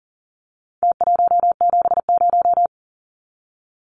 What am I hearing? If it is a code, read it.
Morse code, heard as a keyed tone, about 40 wpm, 700 Hz. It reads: T170
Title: Numbers - Contest Style - 4-Digit 40wpm